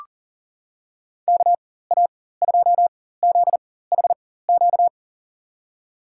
Morse code, heard as a keyed tone, about 40 wpm, spaced as for 19 wpm, 700 Hz. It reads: KA2ZHQ
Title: USA Call Signs - 4x Character Spacing - 40wpm